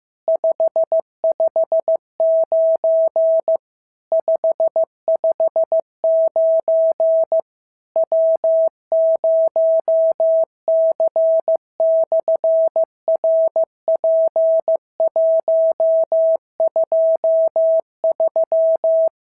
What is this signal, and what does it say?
Morse code, heard as a keyed tone, about 15 wpm, 650 Hz. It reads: 559 559 W0C/RP123
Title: SOTA QSO Practice 15wpm